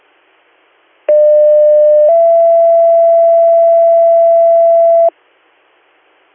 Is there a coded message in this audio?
Yes: two-tone page from 1.1 to 5.1 s, heard as a one-second tone, then a three-second tone, about 600 then 670 Hz.